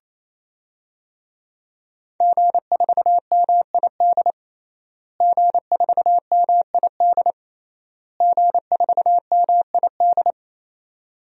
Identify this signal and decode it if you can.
Morse code, heard as a keyed tone, about 28 wpm, 700 Hz. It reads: G4MID G4MID G4MID
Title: DXCC-100 Call Signs - 5 Characters - Repeated 3x - 28wpm